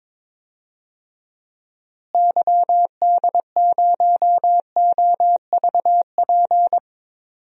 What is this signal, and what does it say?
Morse code, heard as a keyed tone, about 22 wpm, 700 Hz. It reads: YD0OVP
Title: DXCC-100 Call Signs - 6 Characters - Repeated 3x - 22wpm